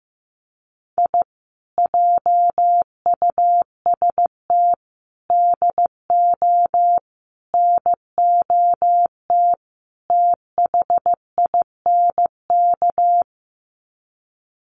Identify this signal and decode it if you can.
Morse code, heard as a keyed tone, about 15 wpm, 700 Hz. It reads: I JUST DO NOT THINK
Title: Sets of 5 Words Encore 15wpm